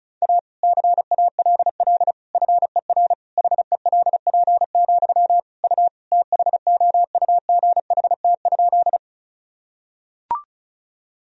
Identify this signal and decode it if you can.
Morse code, heard as a keyed tone, about 36 wpm, 700 Hz. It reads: A CALL FER HELP, U THOUGHT?
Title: Sentences from Top 300 Words and Abbreviations 35wpm